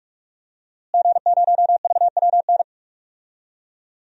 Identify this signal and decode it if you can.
Morse code, heard as a keyed tone, about 46 wpm, 700 Hz. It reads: G0VWN
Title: DXCC-100 Call Signs - 5 Characters - Repeated 2x - 45wpm